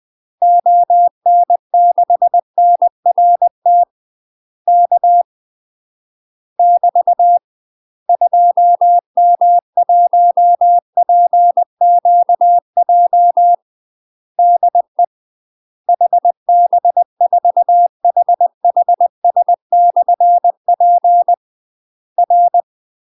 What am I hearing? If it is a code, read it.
Morse code, heard as a keyed tone, about 20 wpm, 700 Hz. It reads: ON6NRT K <BT> 2M1PQJ DE HB4HHS/P R
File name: UK MASTER QSOs TRAINING / 20WPM / UK MASTER QSOs 0055